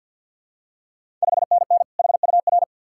Morse code, heard as a keyed tone, about 53 wpm, 690 Hz. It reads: 5NN HUR
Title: Ontario QSO Party Exchange 50wpm